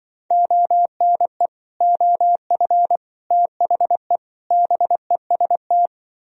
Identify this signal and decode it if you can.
Morse code, heard as a keyed tone, about 24 wpm, 700 Hz. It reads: ONE OF THE BEST
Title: Speed-Racing - Sets of 4 Words 20wpm